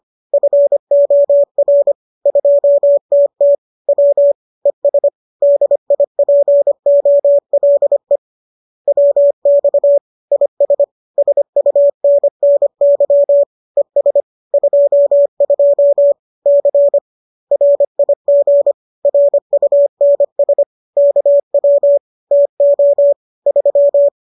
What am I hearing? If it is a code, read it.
Morse code, heard as a keyed tone, about 25 wpm, 570 Hz. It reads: FOR 2TT W ES DIPOLE WX IS SUNNY ES 22 C RIG RUNS KW TO 3